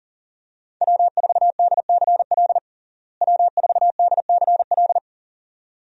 Morse code, heard as a keyed tone, about 40 wpm, 700 Hz. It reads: W4DCL W4DCL
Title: USA Call Signs - 5 Characters - Repeated 2x - 40wpm